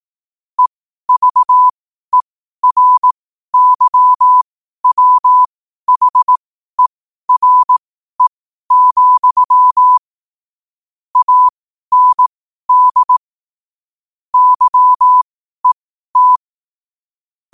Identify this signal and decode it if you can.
Morse code, heard as a keyed tone, about 18 wpm, spaced as for 12 wpm, 1000 Hz. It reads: EVERYWHERE, AND YET
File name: qotdfiles / 12wpm